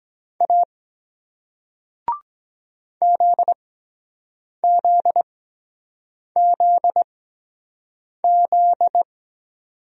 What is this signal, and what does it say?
Morse code, heard as a keyed tone, about 23 wpm, spaced as for 7 wpm, 700 Hz. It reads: A ZZZZ